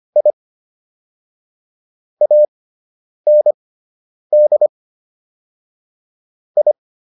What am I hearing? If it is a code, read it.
Morse code, heard as a keyed tone, about 25 wpm, spaced as for 9 wpm, 600 Hz. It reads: I AND I